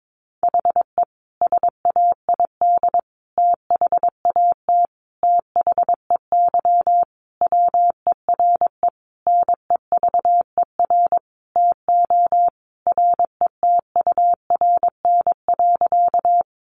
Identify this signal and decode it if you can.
Morse code, heard as a keyed tone, about 22 wpm, 700 Hz. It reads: HE SAID THAT THEY WERE NEVER TO RETURN.
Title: Sentences from Top 800 Words 22wpm